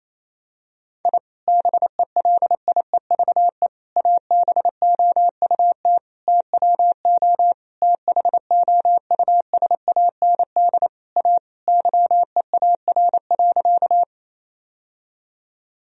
Morse code, heard as a keyed tone, about 27 wpm, 700 Hz. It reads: I BELIEVE ABOUT TWO THOUSAND A YEAR.
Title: Sentences from Top 1000 Words - Review Part 3 - 28wpm